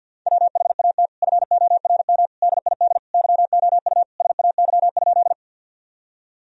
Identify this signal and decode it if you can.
Morse code, heard as a keyed tone, about 51 wpm, 700 Hz. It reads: WHAT FORM DID YOU SAY?
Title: Sentences from Top 1000 Words - Review Part 1 - 50wpm